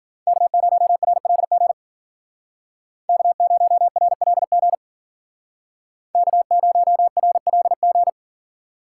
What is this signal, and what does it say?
Morse code, heard as a keyed tone, about 49 wpm, 700 Hz. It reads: K0RLG K0RLG K0RLG
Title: Speed Racing - USA 1x3 Call Signs 40wpm